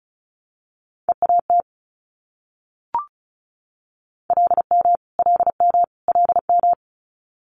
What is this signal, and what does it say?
Morse code, heard as a keyed tone, about 36 wpm, 700 Hz. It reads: EAT LM LM LM
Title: Multi-letter Phonemes Repeated 3x 35wpm